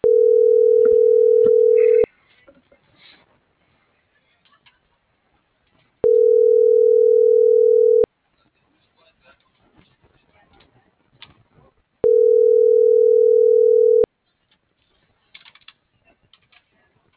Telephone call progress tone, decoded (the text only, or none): ringback tone